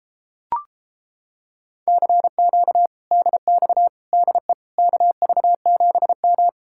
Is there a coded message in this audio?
Yes: Morse code from 1.9 to 6.5 s, heard as a keyed tone, about 29 wpm, 700 Hz. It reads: CQ DX DE KV7M